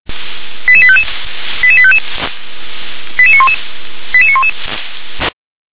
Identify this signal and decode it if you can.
Five-tone selective call (ZVEI), heard as quick runs of five tones, tones of about 70 ms. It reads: 88855 88855 88911 88911